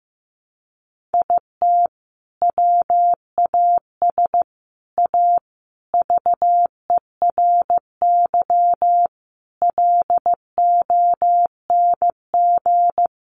Morse code, heard as a keyed tone, about 15 wpm, 700 Hz. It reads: IT WAS A VERY LONG
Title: Sentences from Top 1200 Words 15wpm